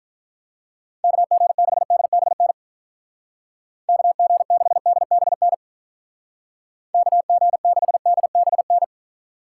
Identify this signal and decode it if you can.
Morse code, heard as a keyed tone, about 44 wpm, 700 Hz. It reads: KG6DBN KG6DBN KG6DBN